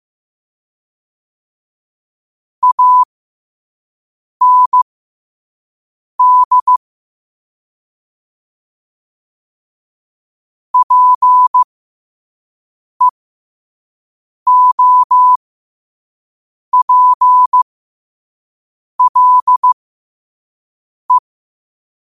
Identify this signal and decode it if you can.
Morse code, heard as a keyed tone, about 15 wpm, spaced as for 5 wpm, 1000 Hz. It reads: AND PEOPLE
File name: qotdfiles / 5wpm